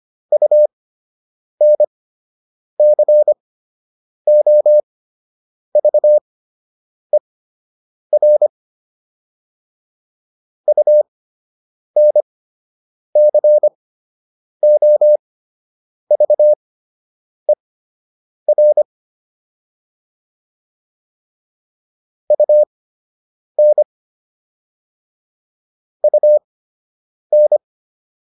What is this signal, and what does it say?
Morse code, heard as a keyed tone, about 25 wpm, spaced as for 8 wpm, 600 Hz. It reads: UNCOVER UNCOVER UN UN